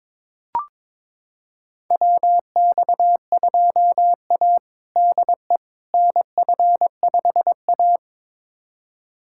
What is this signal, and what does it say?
Morse code, heard as a keyed tone, about 22 wpm, 700 Hz. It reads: WX2A DE NF5A
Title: USA Callsign DE Callsign - 4 Characters - 22wpm